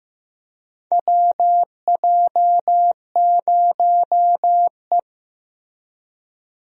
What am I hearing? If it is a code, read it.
Morse code, heard as a keyed tone, about 15 wpm, 700 Hz. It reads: WJ0E
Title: USA Call Signs - 4 Characters - 15wpm